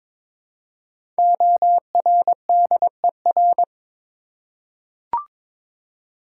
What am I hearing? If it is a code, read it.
Morse code, heard as a keyed tone, about 22 wpm, 700 Hz. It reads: ORDER
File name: 5 Letter Words - First 3 Letters Given - 22wpm